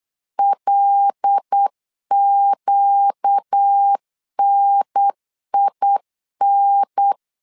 Morse code, heard as a keyed tone, about 8 wpm, 800 Hz. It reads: LQNIN
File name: LEZIONI / 5 LBVC / 5 LB 8wpm 2816